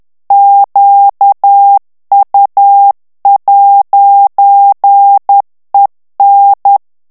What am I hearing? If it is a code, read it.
Morse code, heard as a keyed tone, about 10 wpm, 800 Hz. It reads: QU'EN
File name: morse mp3 / FRANCAIS TEXTE CLAIR 2 VITESSE 10 MOTS MINUTES